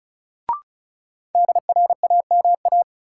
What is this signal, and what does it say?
Morse code, heard as a keyed tone, about 36 wpm, 700 Hz. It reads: DRAMA
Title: Sentences from Top 1500 Words - New Word Drill Alt - 35wpm